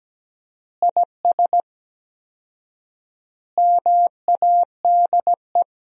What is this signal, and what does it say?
Morse code, heard as a keyed tone, about 17 wpm, 700 Hz. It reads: IS MADE
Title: 4x Word-Spacing - Sentences from Top 500 Words - 17wpm